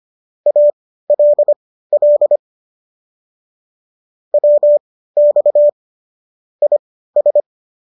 Morse code, heard as a keyed tone, about 25 wpm, spaced as for 15 wpm, 590 Hz. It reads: ALL WX IS